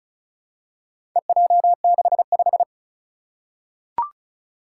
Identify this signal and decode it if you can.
Morse code, heard as a keyed tone, about 36 wpm, 700 Hz. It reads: EJ65
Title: Four Challenging Letter-Number Combinations 35wpm